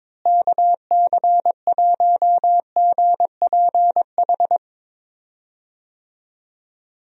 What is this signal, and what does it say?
Morse code, heard as a keyed tone, about 22 wpm, 700 Hz. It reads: KC1GPH
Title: USA Call Signs - 6 Characters - Repeated 3x - 22wpm